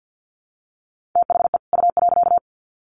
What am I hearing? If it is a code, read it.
Morse code, heard as a keyed tone, about 50 wpm, 700 Hz. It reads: THE U.
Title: Sets of 2 Words 50wpm